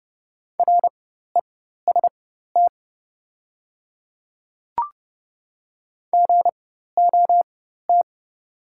Morse code, heard as a keyed tone, about 30 wpm, spaced as for 14 wpm, 700 Hz. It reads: REST GOT
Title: 4x Character Spacing - Top 500 Words - 30wpm